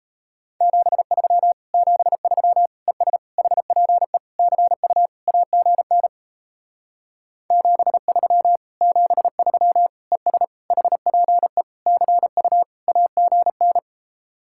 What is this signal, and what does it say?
Morse code, heard as a keyed tone, about 35 wpm, 700 Hz. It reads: 73 73 ES HPE CU AGN 73 73 ES HPE CU AGN